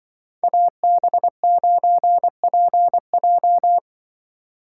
Morse code, heard as a keyed tone, about 23 wpm, 700 Hz. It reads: AB9PJ
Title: Speed Racing - USA 2x2 Call Signs 20wpm